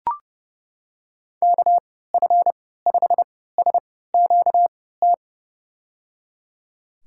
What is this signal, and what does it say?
Morse code, heard as a keyed tone, about 31 wpm, spaced as for 17 wpm, 700 Hz. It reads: KF5SQT
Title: USA Call Signs - 3x Character Spacing - 30wpm